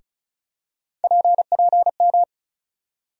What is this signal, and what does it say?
Morse code, heard as a keyed tone, about 34 wpm, 700 Hz. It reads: PPM
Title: USA Call Sign Postfixes - 3 Letters 35wpm